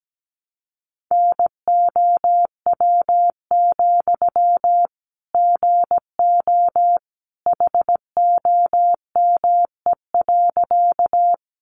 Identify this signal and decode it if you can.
Morse code, heard as a keyed tone, about 17 wpm, 700 Hz. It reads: NOW, GO HOME.
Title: Sentences from Top 200 Words 17wpm